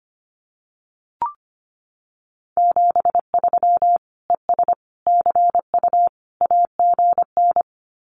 Morse code, heard as a keyed tone, about 25 wpm, 700 Hz. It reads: 73 ES CU AGN
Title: Common QSO Phrases Encore 25wpm